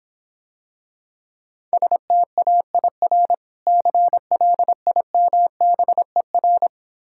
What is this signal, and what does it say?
Morse code, heard as a keyed tone, about 26 wpm, 700 Hz. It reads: STAIR CLIMBER